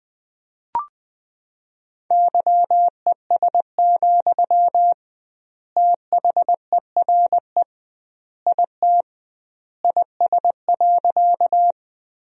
Morse code, heard as a keyed tone, about 21 wpm, 700 Hz. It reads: YES, THERE IT IS.
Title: 2x Word-Spacing - Sentences from Top 500 Words - 20wpm